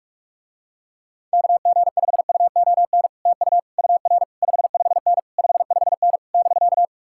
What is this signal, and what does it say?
Morse code, heard as a keyed tone, about 44 wpm, 700 Hz. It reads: KG5UON TU UR 55N 55N <BK>